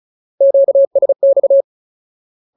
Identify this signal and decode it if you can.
Morse code, heard as a keyed tone, about 34 wpm, 550 Hz. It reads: QSX
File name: Ultimate Word List / 35WPM Ultimate